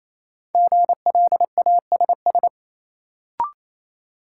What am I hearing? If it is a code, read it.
Morse code, heard as a keyed tone, about 28 wpm, 700 Hz. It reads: GLASS
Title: Sentences from Top 1000 Words - Review Part 3 - New Word Drill - 28wpm